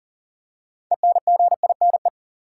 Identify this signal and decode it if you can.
Morse code, heard as a keyed tone, about 40 wpm, 700 Hz. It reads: ENGINE
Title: Six Letter Words - 40wpm